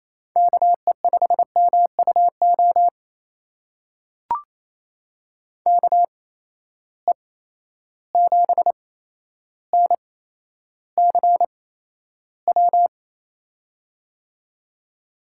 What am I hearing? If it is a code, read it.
Morse code, heard as a keyed tone, about 28 wpm, spaced as for 12 wpm, 700 Hz. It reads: KE5MUO KE7NCW
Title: USA Call Signs - 6 Characters - 8x Character Spacing - 28wpm